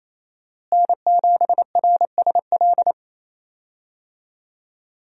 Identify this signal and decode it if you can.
Morse code, heard as a keyed tone, about 28 wpm, 700 Hz. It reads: N7RSL
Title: USA Call Signs - 5 Characters - Repeated 3x - 28wpm